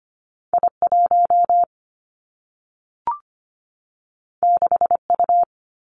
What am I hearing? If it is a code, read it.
Morse code, heard as a keyed tone, about 25 wpm, 700 Hz. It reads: I1 6U
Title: Two Challenging Letter-Number Combinations 25wpm